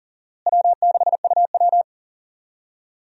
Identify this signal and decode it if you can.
Morse code, heard as a keyed tone, about 40 wpm, 700 Hz. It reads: W6UW
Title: USA Call Signs - 4 Characters - Repeated 2x - 40wpm